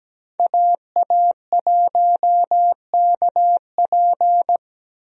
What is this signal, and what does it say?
Morse code, heard as a keyed tone, about 17 wpm, 700 Hz. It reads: AA1KP
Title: USA Call Signs - 5 Characters - 17wpm